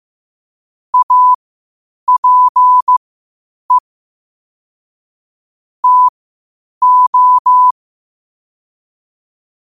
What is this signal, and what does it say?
Morse code, heard as a keyed tone, about 15 wpm, spaced as for 8 wpm, 1000 Hz. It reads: APE TO
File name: qotdfiles / 8wpm